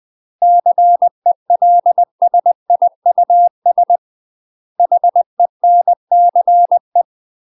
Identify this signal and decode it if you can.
Morse code, heard as a keyed tone, about 20 wpm, 700 Hz. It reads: CELSIUS HENCE
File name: UK MASTER QSOs TRAINING / 20WPM / UK MASTER QSOs 0032